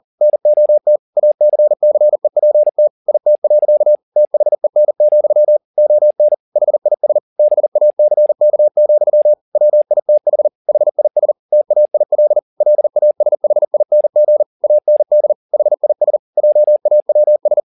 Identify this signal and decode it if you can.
Morse code, heard as a keyed tone, about 40 wpm, 600 Hz. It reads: NOT ACCEPT IT. THEN, ON HIS BACK, WITH HIS TAIL LASHING AND HIS JAWS
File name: the old man and the sea 2 40wpm